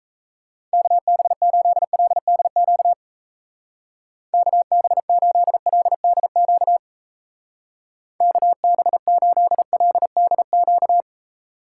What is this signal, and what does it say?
Morse code, heard as a keyed tone, about 34 wpm, 700 Hz. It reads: KB8LDQ KB8LDQ KB8LDQ
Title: Speed Racing - USA Call Signs 28wpm